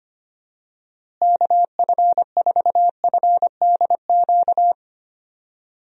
Morse code, heard as a keyed tone, about 25 wpm, 700 Hz. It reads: KF4FDQ
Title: USA Call Signs - 6 Characters - Repeated 3x - 25wpm